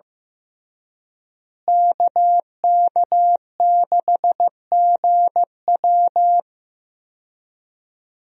Morse code, heard as a keyed tone, about 15 wpm, 700 Hz. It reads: KK6GW
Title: USA Call Signs - 5 Characters - Repeated 3x - 15wpm